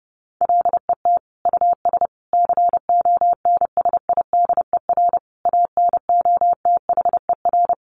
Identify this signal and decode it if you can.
Morse code, heard as a keyed tone, about 30 wpm, 700 Hz. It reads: LET US CONSIDER ANOTHER
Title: Sentences from Top 600 Words 30wpm